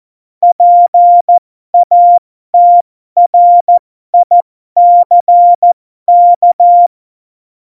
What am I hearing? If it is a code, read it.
Morse code, heard as a keyed tone, about 14 wpm, spaced as for 12 wpm, 700 Hz. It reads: PATRICK
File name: Morsekurs / Witze